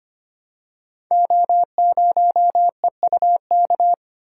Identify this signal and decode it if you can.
Morse code, heard as a keyed tone, about 25 wpm, 700 Hz. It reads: O0EUK
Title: Code Groups - 5 Letter:Numbers - 25wpm